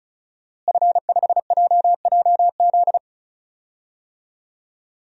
Morse code, heard as a keyed tone, about 36 wpm, 700 Hz. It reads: F5JJZ
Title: DXCC-100 Call Signs - 5 Characters - Repeated 3x - 35wpm